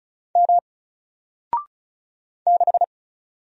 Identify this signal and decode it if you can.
Morse code, heard as a keyed tone, about 36 wpm, 700 Hz. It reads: M 6
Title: Single Letter-Number Alt 35wpm